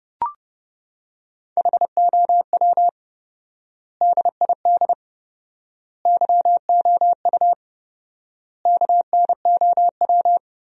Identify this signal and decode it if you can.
Morse code, heard as a keyed tone, about 30 wpm, 700 Hz. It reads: HOW DID YOU KNOW
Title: 4x Word-Spacing - Sentences from Top 100 Words - 30wpm